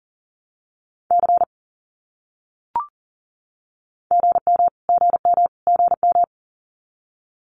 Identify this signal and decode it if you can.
Morse code, heard as a keyed tone, about 40 wpm, 700 Hz. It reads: C GM GM GM
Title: Abbreviations Repeated 3x - 40wpm